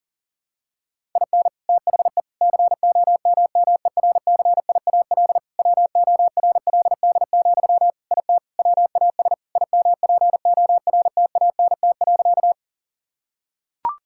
Morse code, heard as a keyed tone, about 40 wpm, 700 Hz. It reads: IN THE COMMERCIAL WORLD, IT WAS IMPORTANT.
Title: Sentences from Top 1000 Words - Review Part 5 - 40wpm